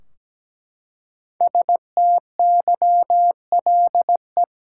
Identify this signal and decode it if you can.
Morse code, heard as a keyed tone, about 17 wpm, 700 Hz. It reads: STYLE
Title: Five Letter Words - 17wpm